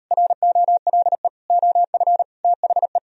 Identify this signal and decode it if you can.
Morse code, heard as a keyed tone, about 38 wpm, 700 Hz. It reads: ROLE OF THE